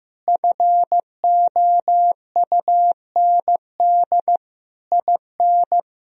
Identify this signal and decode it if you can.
Morse code, heard as a keyed tone, about 15 wpm, 700 Hz. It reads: FOUND IN
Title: Sentences from Top 1000 Words - Review Part 1 - 15wpm